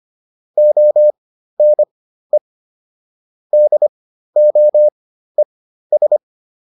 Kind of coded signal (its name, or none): Morse code